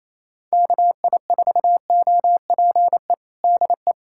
Morse code, heard as a keyed tone, about 28 wpm, 700 Hz. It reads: KI4OPE DE
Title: USA Callsign DE Callsign 28wpm